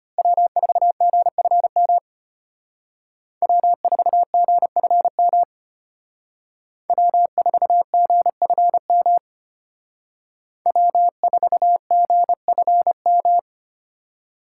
Morse code, heard as a keyed tone, about 32 wpm, 700 Hz. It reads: W4GFM W4GFM W4GFM W4GFM